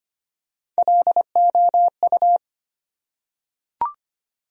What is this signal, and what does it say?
Morse code, heard as a keyed tone, about 25 wpm, 700 Hz. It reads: LOU
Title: USA Common Names - Max 4 Letters - 25wpm